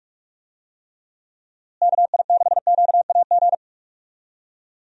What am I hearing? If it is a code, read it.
Morse code, heard as a keyed tone, about 48 wpm, 700 Hz. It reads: KI6QAG
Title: USA Call Signs - 6 Characters - Repeated 3x - 45wpm